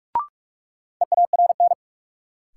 Morse code, heard as a keyed tone, about 41 wpm, 700 Hz. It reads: EARN
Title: Sentences from Top 1500 Words - New Word Drill - 45wpm